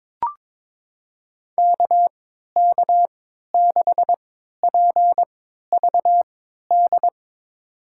Morse code, heard as a keyed tone, about 22 wpm, spaced as for 13 wpm, 700 Hz. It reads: KK6PVD
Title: USA Call Signs - 6 Characters - 3x Character Spacing - 22wpm